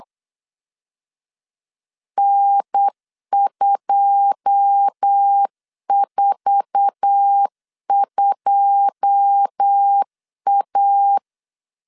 Morse code, heard as a keyed tone, about 8 wpm, 800 Hz. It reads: N242A
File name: LEZIONI / 9 2 8wpm 3437